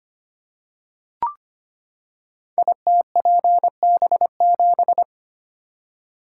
Morse code, heard as a keyed tone, about 25 wpm, 700 Hz. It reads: ITPB7
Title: Code Groups - 5 Letter:Numbers - 25wpm